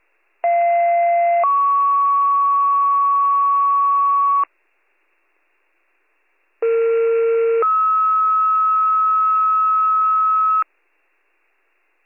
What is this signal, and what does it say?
Two-tone page, heard as a one-second tone, then a three-second tone, about 690 then 1120 Hz, about 460 then 1290 Hz. It is heard from 0.4 to 10.6 s.